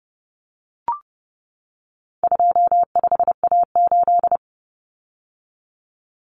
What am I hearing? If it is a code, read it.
Morse code, heard as a keyed tone, about 30 wpm, 700 Hz. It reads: 25A8
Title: Common Cut Numbers - 4-Digit 30wpm